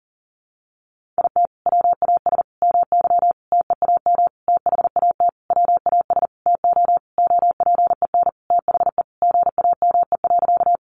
Morse code, heard as a keyed tone, about 40 wpm, 700 Hz. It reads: IT WAS MY TEAM THAT WAS TO OPEN THE GAME.
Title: Sentences from Top 1500 Words 40wpm